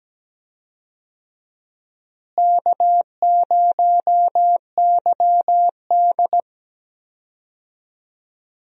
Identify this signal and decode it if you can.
Morse code, heard as a keyed tone, about 17 wpm, 700 Hz. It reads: K0YD
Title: USA Call Signs - 4 Characters - Repeated 3x - 17wpm